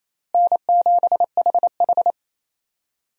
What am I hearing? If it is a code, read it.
Morse code, heard as a keyed tone, about 28 wpm, 700 Hz. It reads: N7HH
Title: USA Call Signs - 4 Characters - Repeated 2x - 28wpm